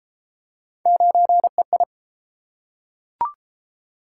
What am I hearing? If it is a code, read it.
Morse code, heard as a keyed tone, about 31 wpm, 700 Hz. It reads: 9EI